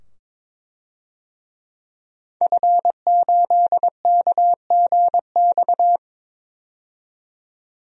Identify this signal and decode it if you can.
Morse code, heard as a keyed tone, about 22 wpm, 700 Hz. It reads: F8KGX
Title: DXCC-100 Call Signs - 5 Characters - 22wpm